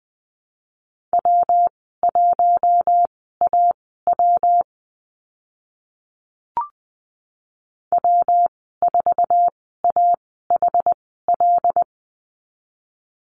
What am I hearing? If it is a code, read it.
Morse code, heard as a keyed tone, about 20 wpm, spaced as for 14 wpm, 700 Hz. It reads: W1AW W4AHL
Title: 122 Lesson 20g - Number 3 - Call Signs 20wpm